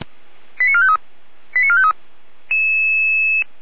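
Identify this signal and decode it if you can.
Five-tone selective call (ZVEI), heard as quick runs of five tones, tones of about 70 ms. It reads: 89453 89453
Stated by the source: ZVEI five-tone selective call 89160 received by SDR, from the Feuerwehr Orlamünde fire brigade audio server.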